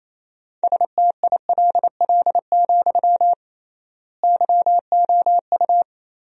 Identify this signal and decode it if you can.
Morse code, heard as a keyed tone, about 28 wpm, 700 Hz. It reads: STILL, YOU
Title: 3x Word-Spacing - Sentences from Top 300 Words - 28wpm